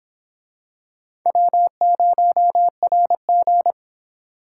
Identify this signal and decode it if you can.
Morse code, heard as a keyed tone, about 25 wpm, 700 Hz. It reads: W0RG